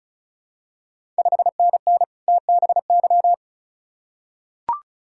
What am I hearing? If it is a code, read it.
Morse code, heard as a keyed tone, about 36 wpm, 700 Hz. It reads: 5NN TBY
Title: Ontario QSO Party Exchange 35wpm